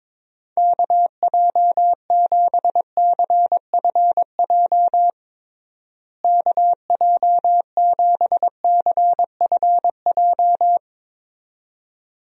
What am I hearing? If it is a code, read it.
Morse code, heard as a keyed tone, about 22 wpm, 700 Hz. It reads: KJ7CFJ KJ7CFJ
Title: USA Call Signs - 6 Characters - Repeated 2x - 22wpm